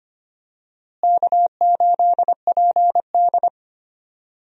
Morse code, heard as a keyed tone, about 25 wpm, 700 Hz. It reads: K8PD
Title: USA Call Signs - 4 Characters - Repeated 3x - 25wpm